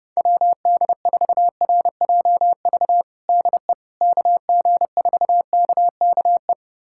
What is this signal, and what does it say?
Morse code, heard as a keyed tone, about 30 wpm, 700 Hz. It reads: WD4RJV DE KG4KKE
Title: USA Callsign DE Callsign - 6 Characters - 30wpm